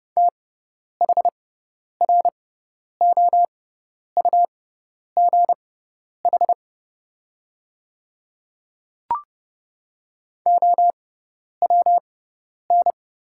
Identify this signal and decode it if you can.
Morse code, heard as a keyed tone, about 30 wpm, spaced as for 10 wpm, 700 Hz. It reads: THROUGH OWN
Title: 6x Character Spacing - Top 300 Words - 30wpm